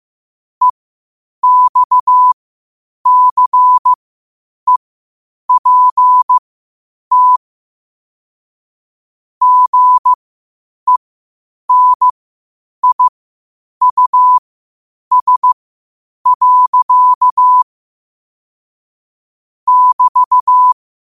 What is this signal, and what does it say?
Morse code, heard as a keyed tone, about 15 wpm, spaced as for 8 wpm, 1000 Hz. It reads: EXCEPT GENIUS. <BT>
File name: qotdfiles / 8wpm